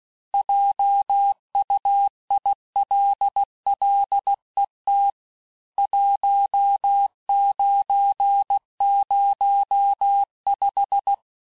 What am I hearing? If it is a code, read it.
Morse code, heard as a keyed tone, about 16 wpm, 800 Hz. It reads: JUILLET 1905
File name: texte-clair-vitesse-15mpm-puis-20mpm